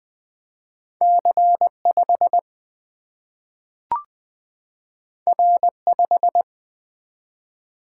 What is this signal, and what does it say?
Morse code, heard as a keyed tone, about 20 wpm, 700 Hz. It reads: C5 R5
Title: Single-Letter Single-Number 20wpm